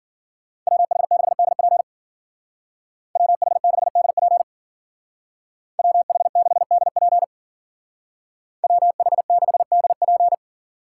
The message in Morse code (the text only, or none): WH6DP WH6DP WH6DP WH6DP